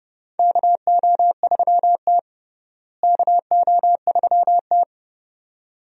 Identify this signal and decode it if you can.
Morse code, heard as a keyed tone, about 32 wpm, 700 Hz. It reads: KO3T KO3T
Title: USA Call Signs - 4 Characters - Repeated 2x - 30wpm